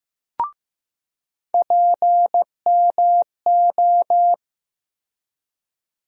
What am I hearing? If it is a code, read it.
Morse code, heard as a keyed tone, about 15 wpm, 700 Hz. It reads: PMO
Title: USA Call Sign Postfixes - 3 Letters 15wpm